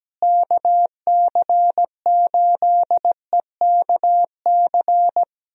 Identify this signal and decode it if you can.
Morse code, heard as a keyed tone, about 17 wpm, 700 Hz. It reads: KC8EKC
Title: USA Call Signs - 6 Characters - Repeated 2x - 17wpm